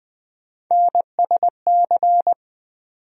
Morse code, heard as a keyed tone, about 20 wpm, 700 Hz. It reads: NSC
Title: USA Call Sign Postfixes - 3 Letters 20wpm